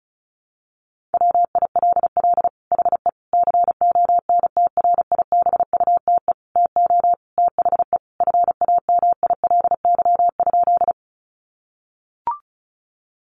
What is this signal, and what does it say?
Morse code, heard as a keyed tone, about 34 wpm, 700 Hz. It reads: WILL HE CONTRIBUTE TO THE FAMILY?
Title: Sentences from Top 1400 Words 35wpm